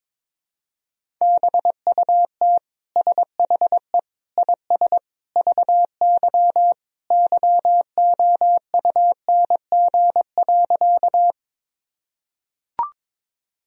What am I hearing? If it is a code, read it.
Morse code, heard as a keyed tone, about 22 wpm, 700 Hz. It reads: BUT SHE IS VY YOUNG.
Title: Sentences from Top 400 Words and Abbreviations 22wpm